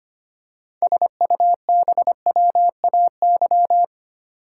Morse code, heard as a keyed tone, about 28 wpm, 700 Hz. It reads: SUBWAY
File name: Easy Words - Speed-Racing with Hints - 25wpm